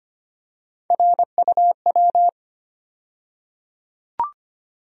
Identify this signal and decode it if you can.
Morse code, heard as a keyed tone, about 25 wpm, 700 Hz. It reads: RUW
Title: USA Call Sign Postfixes - 3 Letters 25wpm